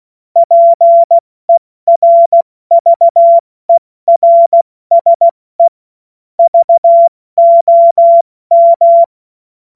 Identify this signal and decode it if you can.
Morse code, heard as a keyed tone, about 16 wpm, 680 Hz. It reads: PERVERSE VOM
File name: Morsekurs / Witze / Witz 25. April 70 F80